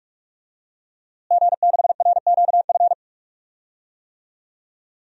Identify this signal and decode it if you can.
Morse code, heard as a keyed tone, about 46 wpm, 700 Hz. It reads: G6RQF
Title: DXCC-100 Call Signs - 5 Characters - Repeated 3x - 45wpm